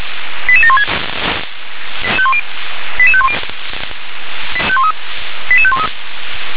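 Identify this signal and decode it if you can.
Five-tone selective call (ZVEI), heard as quick runs of five tones, tones of about 70 ms. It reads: 88716 88510 88513
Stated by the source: ZVEI five-tone selective call received on SDR.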